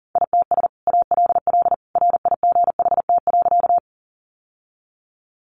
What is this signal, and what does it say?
Morse code, heard as a keyed tone, about 40 wpm, 700 Hz. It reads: ITS ALL RIGHT.
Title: Sentences from Top 600 Words 40wpm